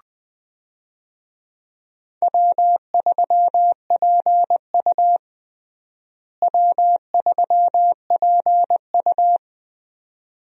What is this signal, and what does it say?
Morse code, heard as a keyed tone, about 20 wpm, 700 Hz. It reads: W3PU W3PU
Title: USA Call Signs - 4 Characters - Repeated 2x - 20wpm